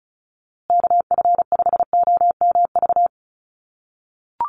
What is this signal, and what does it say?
Morse code, heard as a keyed tone, about 34 wpm, 700 Hz. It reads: KF5OMV
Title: USA Call Signs Repeated 2x 35wpm